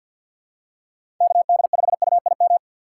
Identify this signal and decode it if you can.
Morse code, heard as a keyed tone, about 52 wpm, 700 Hz. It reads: KD5UIM
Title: USA Call Signs - 6 Characters - 3x Character Spacing - 50wpm